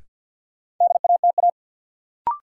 Morse code, heard as a keyed tone, about 52 wpm, 700 Hz. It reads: DATA
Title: Sentences from Top 1000 Words - Review Part 3 - New Word Drill - 50wpm